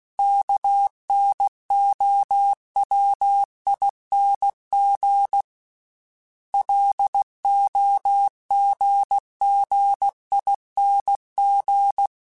Morse code, heard as a keyed tone, about 16 wpm, 800 Hz. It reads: KNOWING LOGGING
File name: Special INGs / Special INGs-mp3s / Sngl ing @15